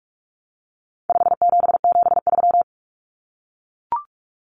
Morse code, heard as a keyed tone, about 44 wpm, 700 Hz. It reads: 5773
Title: Common Cut Numbers - 4-Digit 45wpm